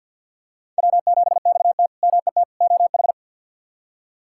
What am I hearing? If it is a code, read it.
Morse code, heard as a keyed tone, about 51 wpm, 700 Hz. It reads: W7XT MET OH
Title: K1USN SST QSO Practice - 50wpm